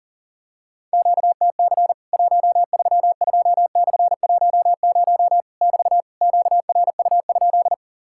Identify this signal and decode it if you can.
Morse code, heard as a keyed tone, about 40 wpm, 700 Hz. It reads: QTC 132/10 <BT> QRU?
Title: Worked All Europe DX Contest - QTC Exchange - 40wpm